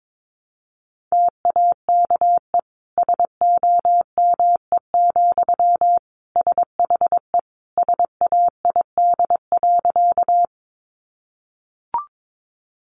Morse code, heard as a keyed tone, about 22 wpm, 700 Hz. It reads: TAKE SOME, SHE SAID.
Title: Sentences from Top 200 Words 22wpm